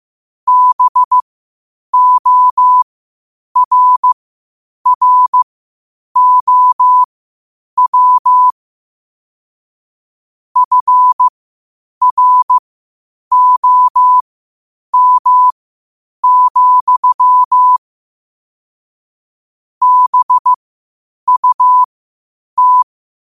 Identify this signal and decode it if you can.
Morse code, heard as a keyed tone, about 15 wpm, spaced as for 8 wpm, 1000 Hz. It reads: BORROW FROM, BUT
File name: qotdfiles / 8wpm